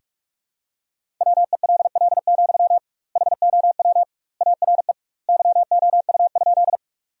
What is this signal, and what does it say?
Morse code, heard as a keyed tone, about 43 wpm, 700 Hz. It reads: WELL, HOW ARE YOU?
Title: 2x Word-Spacing - Sentences from Top 200 Words - 45wpm